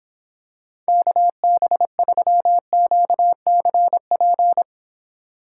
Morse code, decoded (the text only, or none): KB3QCP